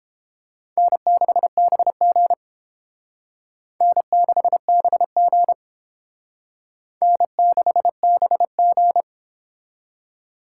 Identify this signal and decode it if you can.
Morse code, heard as a keyed tone, about 28 wpm, 700 Hz. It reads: N6BG N6BG N6BG